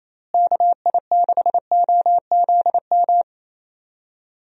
Morse code, heard as a keyed tone, about 28 wpm, 700 Hz. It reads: KI6OZM
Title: USA Call Signs - 6 Characters - Repeated 3x - 28wpm